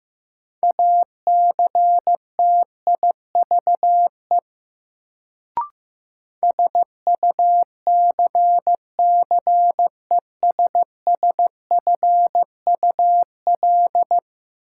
Morse code, heard as a keyed tone, about 15 wpm, 700 Hz. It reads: ACTIVE SUCCESSFUL
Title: Sentences from Top 900 Words - New Word Drill - 15wpm